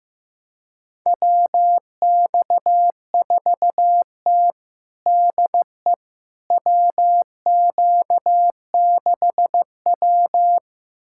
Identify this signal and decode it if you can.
Morse code, heard as a keyed tone, about 15 wpm, 700 Hz. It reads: WX4T DE WQ6W
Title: USA Callsign DE Callsign - 4 Characters - 15wpm